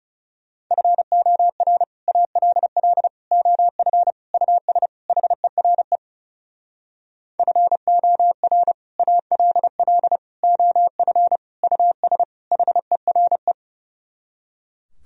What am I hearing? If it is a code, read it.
Morse code, heard as a keyed tone, about 38 wpm, 700 Hz. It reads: FOR ALL OF US HERE FOR ALL OF US HERE